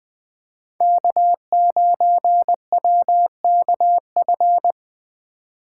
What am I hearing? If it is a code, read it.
Morse code, heard as a keyed tone, about 20 wpm, 700 Hz. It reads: K9WKF
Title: USA Call Signs - 5 Characters - 4x Character Spacing - 20wpm